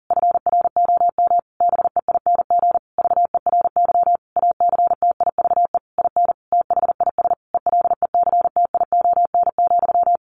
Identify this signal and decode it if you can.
Morse code, heard as a keyed tone, about 37 wpm, 700 Hz. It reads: FROM BEING VERY ACTIVE IN THIS ELECTION,
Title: Sentences from Top 1500 Words 40wpm